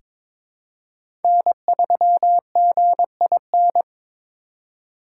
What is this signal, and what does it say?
Morse code, heard as a keyed tone, about 23 wpm, 700 Hz. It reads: N3GIN
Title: USA Call Signs - 3x Character Spacing - 22wpm